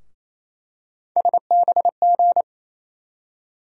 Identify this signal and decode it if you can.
Morse code, heard as a keyed tone, about 28 wpm, 700 Hz. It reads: SBG
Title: USA Call Sign Postfixes - 3 Letters 28wpm